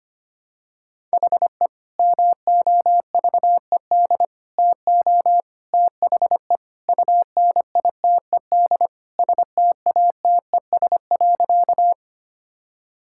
Morse code, heard as a keyed tone, about 25 wpm, 700 Hz. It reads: HE MOVED TO THE UNITED STATES.
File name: practice / 25wpm / Sentences with 3-Word Phrases - 25wpm